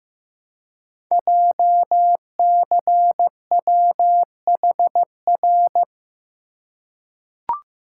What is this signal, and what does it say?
Morse code, heard as a keyed tone, about 15 wpm, 700 Hz. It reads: JCWHR
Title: Code Groups - 5 Letters - 15wpm